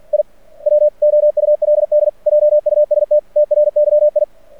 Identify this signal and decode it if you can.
Morse code, heard as a keyed tone, about 50 wpm, 600 Hz. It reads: I WOULD JUST TRYI
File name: archived / CRSnet